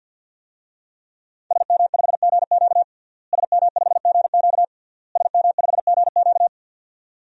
Morse code, heard as a keyed tone, about 51 wpm, 700 Hz. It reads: SM5GQ SM5GQ SM5GQ
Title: DXCC-100 Call Signs - 5 Characters - Repeated 3x - 50wpm